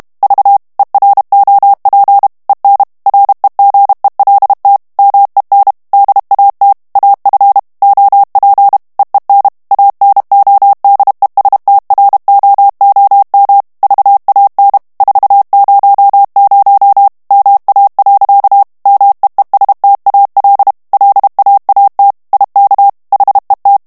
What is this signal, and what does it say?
Morse code, heard as a keyed tone, about 32 wpm, 800 Hz. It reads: V EROP EN REGELT MEN DAT AF OP EEN ANODESTROOM VAN 400 MA. MEESTAL LAAT IK HET
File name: HOLLANDAIS texte clair 2 vitesse 30 mots minute